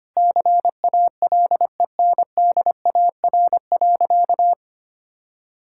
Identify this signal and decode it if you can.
Morse code, heard as a keyed tone, about 25 wpm, 700 Hz. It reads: CALENDAR.
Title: Sentences from Top 1600 Words 25wpm